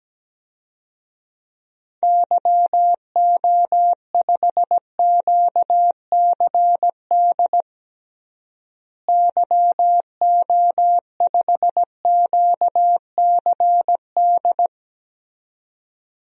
Morse code, heard as a keyed tone, about 17 wpm, 700 Hz. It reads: YO5QCD YO5QCD
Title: DXCC-100 Call Signs - 6 Characters - Repeated 2x - 17wpm